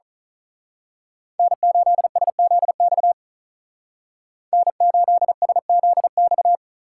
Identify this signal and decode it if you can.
Morse code, heard as a keyed tone, about 36 wpm, 700 Hz. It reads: N8SZX N8SZX
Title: Speed Racing - USA 1x3 Call Signs 35wpm